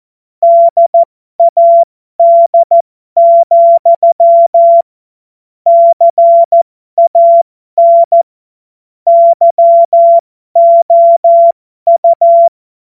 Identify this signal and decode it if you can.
Morse code, heard as a keyed tone, about 14 wpm, spaced as for 12 wpm, 680 Hz. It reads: DAD, CAN YOU
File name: Morsekurs / Witze